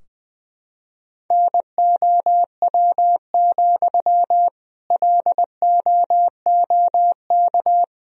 Morse code, heard as a keyed tone, about 19 wpm, 700 Hz. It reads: NOW, LOOK
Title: Sentences from Top 1000 Words - Review Part 4 - 20wpm